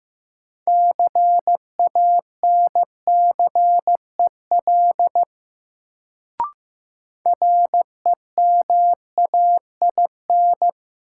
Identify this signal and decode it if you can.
Morse code, heard as a keyed tone, about 15 wpm, 700 Hz. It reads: CANCEL REMAIN
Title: Sentences from Top 1200 Words - New Word Drill - 15wpm